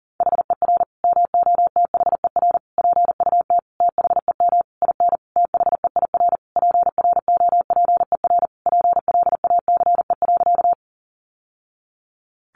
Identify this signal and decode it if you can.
Morse code, heard as a keyed tone, about 40 wpm, 700 Hz. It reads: HER MOTHER PUT THEM IN THEIR PROPER PLACE.
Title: Sentences from Top 800 Words 40wpm